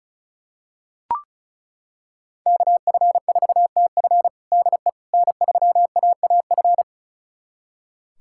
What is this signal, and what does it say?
Morse code, heard as a keyed tone, about 36 wpm, 700 Hz. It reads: KF4TF DE N3AAF
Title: USA Callsign DE Callsign - 5 Characters - 35wpm